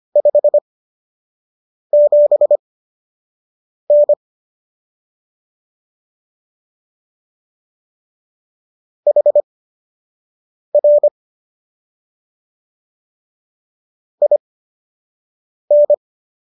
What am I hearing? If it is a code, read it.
Morse code, heard as a keyed tone, about 25 wpm, spaced as for 6 wpm, 590 Hz. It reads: 57N HR IN